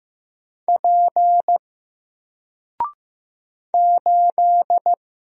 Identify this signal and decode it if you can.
Morse code, heard as a keyed tone, about 15 wpm, 700 Hz. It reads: P 8
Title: Single Letter-Number 15wpm